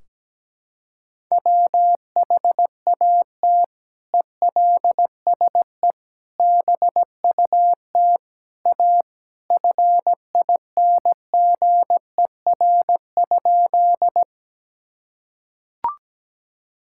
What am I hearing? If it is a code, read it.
Morse code, heard as a keyed tone, about 17 wpm, 700 Hz. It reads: WHAT ELSE BUT A FINGER?
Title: Sentences from Top 1600 Words 17wpm